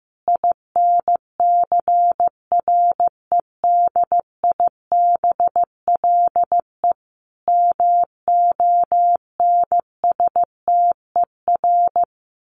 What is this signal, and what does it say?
Morse code, heard as a keyed tone, about 15 wpm, 700 Hz. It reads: INCREDIBLE MONSTER
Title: A Sound Of Thunder 15wpm